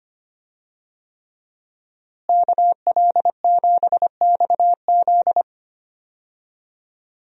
Morse code, heard as a keyed tone, about 25 wpm, 700 Hz. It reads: KL7XZ
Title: DXCC-100 Call Signs - 5 Characters - Repeated 3x - 25wpm